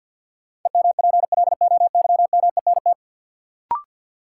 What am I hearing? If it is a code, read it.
Morse code, heard as a keyed tone, about 53 wpm, 700 Hz. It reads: EMPLOYMENT
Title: Sentences from Top 1500 Words - New Word Drill - 50wpm